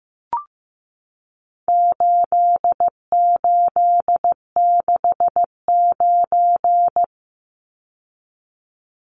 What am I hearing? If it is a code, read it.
Morse code, heard as a keyed tone, about 15 wpm, 700 Hz. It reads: 8869
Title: Numbers - 4-Digit 15wpm